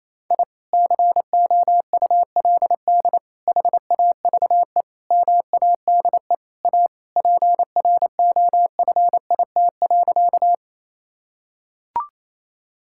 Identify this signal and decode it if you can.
Morse code, heard as a keyed tone, about 28 wpm, 700 Hz. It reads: I COULD HAVE MADE A PROFIT.
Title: Sentences from Top 1000 Words - Review Part 5 - 28wpm